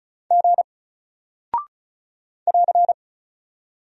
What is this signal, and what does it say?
Morse code, heard as a keyed tone, about 36 wpm, 700 Hz. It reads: G <AR>
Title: Single Character (Letter:Number:Punctuation:Prosign) 35wpm